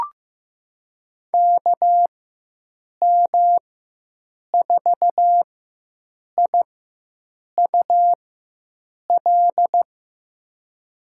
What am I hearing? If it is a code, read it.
Morse code, heard as a keyed tone, about 15 wpm, spaced as for 7 wpm, 700 Hz. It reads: KM4IUL
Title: USA Call Signs - 6 Characters - 4x Character Spacing - 15wpm